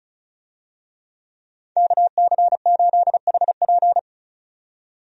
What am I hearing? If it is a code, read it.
Morse code, heard as a keyed tone, about 36 wpm, 700 Hz. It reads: KC8HP
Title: USA Call Signs - 5 Characters - 35wpm